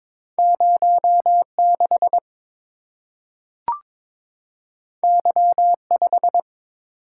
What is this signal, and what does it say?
Morse code, heard as a keyed tone, about 22 wpm, 700 Hz. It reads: 06 Y5
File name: practice / 22wpm / Two Letter-Number Combinations - 22wpm